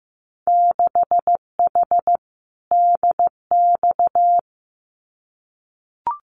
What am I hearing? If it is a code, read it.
Morse code, heard as a keyed tone, about 15 wpm, 700 Hz. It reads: 6H DX
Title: Winter Field Day Exchanges Encore 15wpm